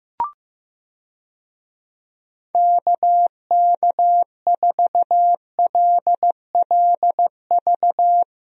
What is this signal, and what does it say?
Morse code, heard as a keyed tone, about 15 wpm, 700 Hz. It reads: KK4LLV
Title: USA Call Signs - 6 Characters - 15wpm